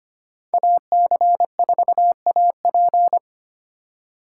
Morse code, heard as a keyed tone, about 25 wpm, 700 Hz. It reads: AC4AP
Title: USA Call Signs - 5 Characters - 2x Character Spacing - 25wpm